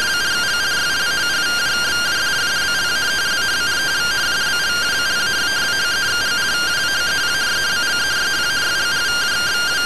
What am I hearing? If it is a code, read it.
Two-tone teleprinter signal, about 170 Hz shift.